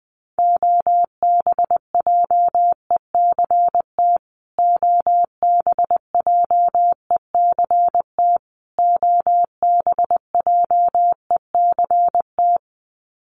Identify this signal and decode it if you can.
Morse code, heard as a keyed tone, about 20 wpm, 700 Hz. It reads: OBJECT OBJECT OBJECT
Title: Top 500 Words Repeated 20wpm